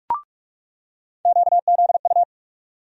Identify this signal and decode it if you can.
Morse code, heard as a keyed tone, about 46 wpm, 700 Hz. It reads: QZU
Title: USA Call Sign Postfixes - 3 Letters 45wpm